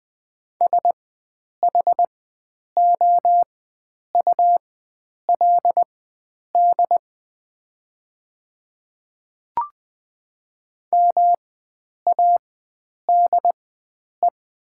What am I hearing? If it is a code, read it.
Morse code, heard as a keyed tone, about 20 wpm, spaced as for 9 wpm, 700 Hz. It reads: SHOULD MADE
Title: 4x Character Spacing - Top 300 Words - 20wpm